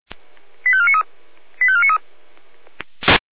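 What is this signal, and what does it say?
Five-tone selective call (ZVEI), heard as quick runs of five tones, tones of about 70 ms. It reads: 85493 85493
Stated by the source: Five-tone ZVEI selective call 85491, received by SDR.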